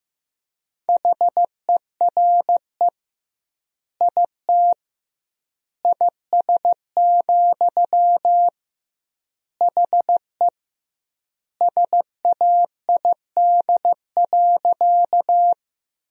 Morse code, heard as a keyed tone, about 15 wpm, 700 Hz. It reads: HERE IT IS, HE SAID.
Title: 2x Word-Spacing - Sentences from Top 200 Words - 15wpm